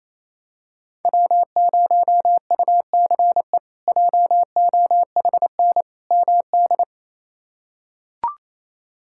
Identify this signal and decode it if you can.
Morse code, heard as a keyed tone, about 28 wpm, 700 Hz. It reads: W0UCE JOHN MD
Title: K1USN SST QSO Practice - 28wpm